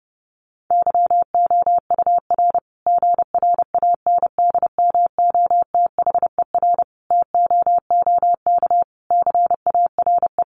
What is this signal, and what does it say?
Morse code, heard as a keyed tone, about 30 wpm, 700 Hz. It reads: YOUR GRANDMOTHER TOOK CARE
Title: Sentences from Top 1200 Words 30wpm